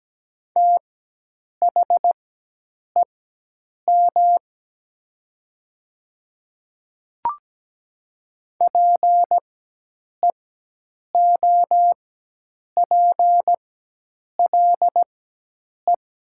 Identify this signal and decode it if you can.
Morse code, heard as a keyed tone, about 17 wpm, spaced as for 8 wpm, 700 Hz. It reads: THEM PEOPLE
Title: 4x Character Spacing - Top 300 Words - 17wpm